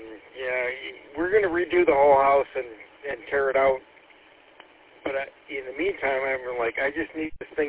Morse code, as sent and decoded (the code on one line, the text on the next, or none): none